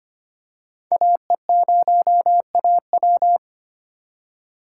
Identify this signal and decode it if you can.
Morse code, heard as a keyed tone, about 25 wpm, 700 Hz. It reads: AE0AW
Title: USA Call Signs - 5 Characters - 4x Character Spacing - 25wpm